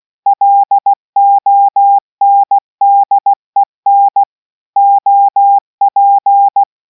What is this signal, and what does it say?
Morse code, heard as a keyed tone, about 16 wpm, 800 Hz. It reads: LONDEN OP